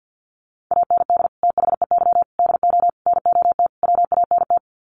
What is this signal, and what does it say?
Morse code, heard as a keyed tone, about 50 wpm, 700 Hz. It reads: AND THEY DO NOT WANT
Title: Sets of 5 Words Encore2 50wpm